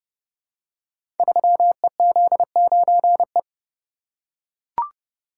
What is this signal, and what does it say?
Morse code, heard as a keyed tone, about 30 wpm, 700 Hz. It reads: 3EZ9E
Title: Code Groups - 5 Letter:Numbers - 30wpm